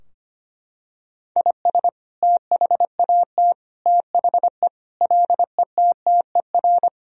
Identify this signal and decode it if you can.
Morse code, heard as a keyed tone, about 25 wpm, 700 Hz. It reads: IS THAT THE LETTER
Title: Sentences from Top 1000 Words - Review Part 2 - 25wpm